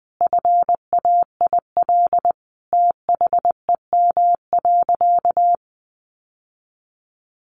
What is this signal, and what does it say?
Morse code, heard as a keyed tone, about 20 wpm, 700 Hz. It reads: FAIL THEM.
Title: Sentences from Top 1100 Words 20wpm